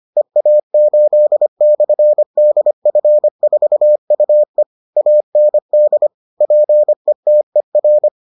Morse code, heard as a keyed tone, about 25 wpm, 600 Hz. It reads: EA8/DF4UE AND PETER